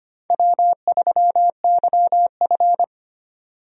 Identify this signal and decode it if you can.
Morse code, heard as a keyed tone, about 25 wpm, 700 Hz. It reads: W3YF
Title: USA Call Signs - 4 Characters - 25wpm